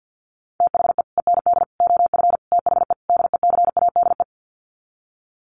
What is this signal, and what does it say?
Morse code, heard as a keyed tone, about 50 wpm, 700 Hz. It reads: THE END OF THE DECADE
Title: Sets of 5 Words Encore 50wpm